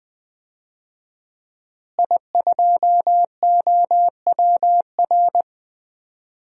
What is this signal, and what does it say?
Morse code, heard as a keyed tone, about 20 wpm, 700 Hz. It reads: I2OWR
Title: DXCC-100 Call Signs - 5 Characters - Repeated 2x - 20wpm